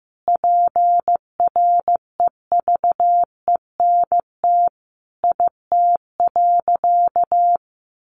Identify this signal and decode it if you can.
Morse code, heard as a keyed tone, about 15 wpm, 700 Hz. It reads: PREVENT IT.
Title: Sentences from Top 1100 Words 15wpm